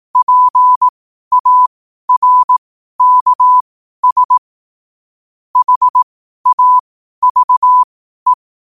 Morse code, heard as a keyed tone, about 18 wpm, spaced as for 12 wpm, 1000 Hz. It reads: PARKS HAVE
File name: qotdfiles / 12wpm